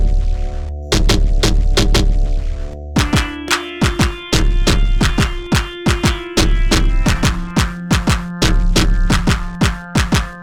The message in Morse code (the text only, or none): none